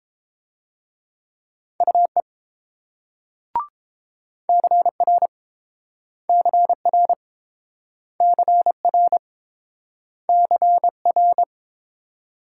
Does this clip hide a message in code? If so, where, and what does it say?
Yes: Morse code from 1.8 to 11.4 s, heard as a keyed tone, about 28 wpm, 700 Hz. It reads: UE CR CR CR CR